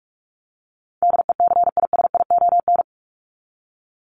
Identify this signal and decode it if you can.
Morse code, heard as a keyed tone, about 44 wpm, 700 Hz. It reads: DECISION
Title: Top 751-1000 Words 45wpm